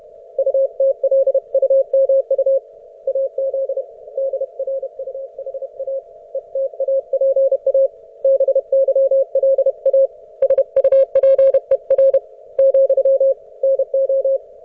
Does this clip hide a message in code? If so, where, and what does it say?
Yes: Morse code from 0.4 to 14.4 s, heard as a keyed tone, about 31 wpm, 540 Hz. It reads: UTLUMU AZ DRUHA ETAPA BYLA SUPER , NO